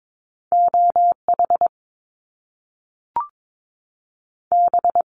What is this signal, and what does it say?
Morse code, heard as a keyed tone, about 22 wpm, 700 Hz. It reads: OH B
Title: Top Two Letter Combinations in Words 22wpm